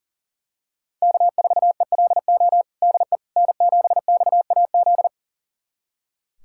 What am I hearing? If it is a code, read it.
Morse code, heard as a keyed tone, about 40 wpm, 700 Hz. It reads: K4ELO DE N7XAZ
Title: USA Callsign DE Callsign 40wpm